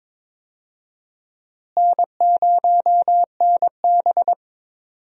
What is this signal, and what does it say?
Morse code, heard as a keyed tone, about 22 wpm, 700 Hz. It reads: N0NB
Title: USA Call Signs - 4 Characters - Repeated 3x - 22wpm